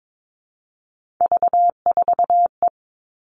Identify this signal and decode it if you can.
Morse code, heard as a keyed tone, about 22 wpm, 700 Hz. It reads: V4E
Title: Three Challenging Letter-Number Combinations 22wpm